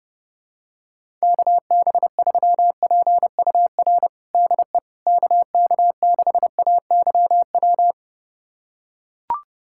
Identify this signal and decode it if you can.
Morse code, heard as a keyed tone, about 30 wpm, 700 Hz. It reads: KB3PUR DE KK6AYW
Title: USA Callsign DE Callsign - 6 Characters - 30wpm